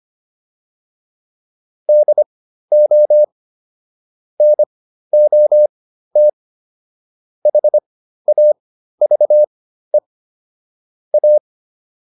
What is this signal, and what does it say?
Morse code, heard as a keyed tone, about 25 wpm, spaced as for 13 wpm, 600 Hz. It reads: DO NOT HAVE A